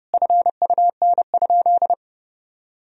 Morse code, heard as a keyed tone, about 30 wpm, 700 Hz. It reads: FUN?
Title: Sentences from Top 1000 Words - Review Part 1 - 30wpm